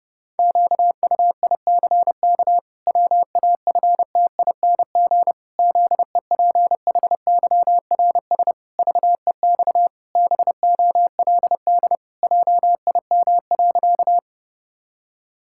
Morse code, heard as a keyed tone, about 30 wpm, 700 Hz. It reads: QUICK WAFTING ZEPHYRS VEX BOLD JIM.
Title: Pangram Challenge v2 - 30wpm